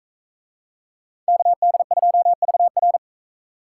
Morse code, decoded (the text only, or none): KD2VR